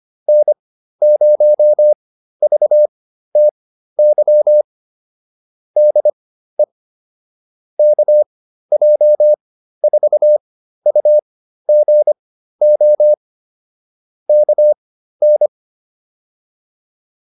Morse code, heard as a keyed tone, about 25 wpm, spaced as for 13 wpm, 600 Hz. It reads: N0VTY DE KJ4UGO KN